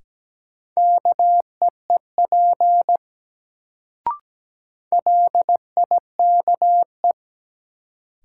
Morse code, heard as a keyed tone, about 17 wpm, 700 Hz. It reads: KEEP LIKE
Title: Sets of 2 Words - New Word Drill - 17wpm